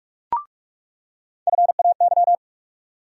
Morse code, heard as a keyed tone, about 46 wpm, 700 Hz. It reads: FAY
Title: USA Call Sign Postfixes - 3 Letters 45wpm